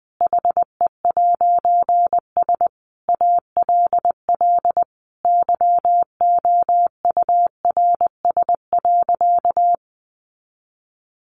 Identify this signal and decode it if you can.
Morse code, heard as a keyed tone, about 19 wpm, 700 Hz. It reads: HE'S ALL YOURS.
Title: A Sound Of Thunder 20wpm